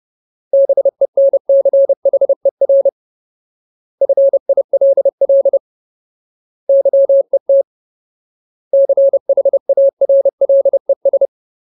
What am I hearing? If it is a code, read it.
Morse code, heard as a keyed tone, about 30 wpm, 550 Hz. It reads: BENCHER FILL YET CHARLES
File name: Ultimate Word List / 30WPM Ultimate